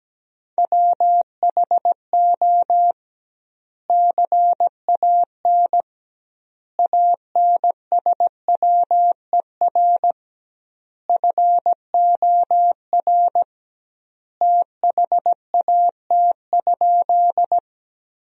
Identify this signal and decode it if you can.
Morse code, heard as a keyed tone, about 17 wpm, 700 Hz. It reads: WHO CAN ANSWER FOR THAT?
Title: 2x Word-Spacing - Sentences from Top 300 Words - 17wpm